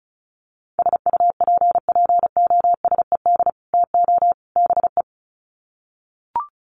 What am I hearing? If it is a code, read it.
Morse code, heard as a keyed tone, about 34 wpm, 700 Hz. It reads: SUPPOSED TO BE
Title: Sets of 3 Words 35wpm